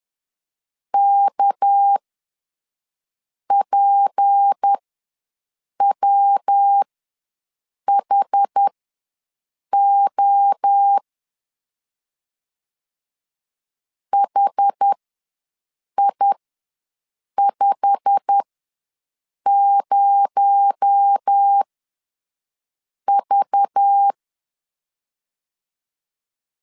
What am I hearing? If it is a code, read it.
Morse code, heard as a keyed tone, about 10 wpm, spaced as for 5 wpm, 800 Hz. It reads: KPWHO HI50V
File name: LEZIONI / 9 28 ws 10wpm 4124